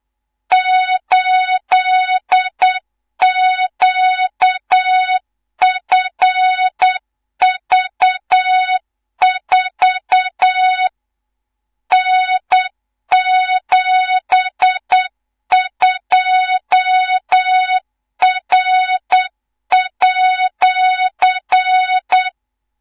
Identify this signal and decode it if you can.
Morse code, heard as a keyed tone, about 8 wpm, 3700 Hz, with noise. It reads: 8QFV4 N72R@
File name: Morze-HA2MN / hang / 18 IRASJELEK2